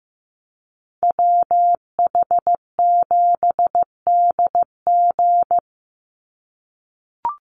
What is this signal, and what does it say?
Morse code, heard as a keyed tone, about 15 wpm, 700 Hz. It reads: WH7DG
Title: USA Call Signs 15wpm